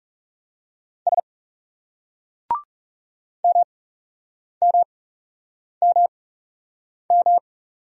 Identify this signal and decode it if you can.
Morse code, heard as a keyed tone, about 39 wpm, spaced as for 8 wpm, 700 Hz. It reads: S MMMM